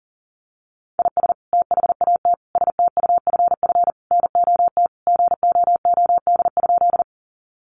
Morse code, heard as a keyed tone, about 40 wpm, 700 Hz. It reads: IS THAT STUFF NOT GOOD?
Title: Sentences from Top 1400 Words 40wpm